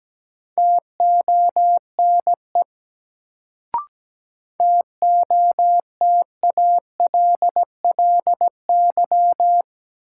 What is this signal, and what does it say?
Morse code, heard as a keyed tone, about 17 wpm, 700 Hz. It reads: TONE TOTALLY
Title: Sentences from Top 1600 Words - New Word Drill - 17wpm